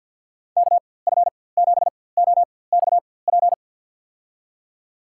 Morse code, heard as a keyed tone, about 53 wpm, spaced as for 24 wpm, 700 Hz. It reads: KF7OXP
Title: USA Call Signs - 6 Characters - 4x Character Spacing - 50wpm